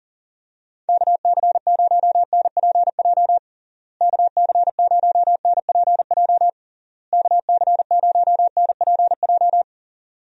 Morse code, heard as a keyed tone, about 40 wpm, 700 Hz. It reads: KC0NPJ KC0NPJ KC0NPJ
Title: USA Call Signs - 6 Characters - Repeated 3x - 40wpm